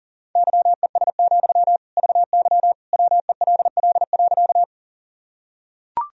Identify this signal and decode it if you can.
Morse code, heard as a keyed tone, about 37 wpm, 700 Hz. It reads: YES, VY WELL.
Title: Sentences from Top 500 Words and Abbreviations 40wpm